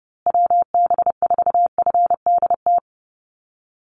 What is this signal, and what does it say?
Morse code, heard as a keyed tone, about 30 wpm, 700 Hz. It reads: WB4FDT
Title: Call Signs Repeated 2x - 30wpm